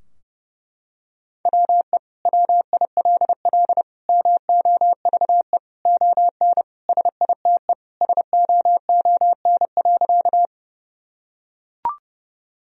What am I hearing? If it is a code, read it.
Morse code, heard as a keyed tone, about 30 wpm, 700 Hz. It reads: WE WILL MOVE ON SITE SOON.
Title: Sentences from Top 1000 Words - Review Part 4 - 30wpm